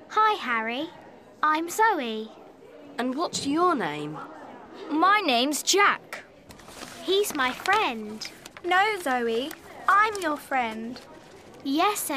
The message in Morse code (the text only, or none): none